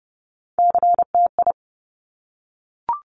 Morse code, heard as a keyed tone, about 30 wpm, 700 Hz. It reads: CTI
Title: Top Three Letter Combinations in Words 30wpm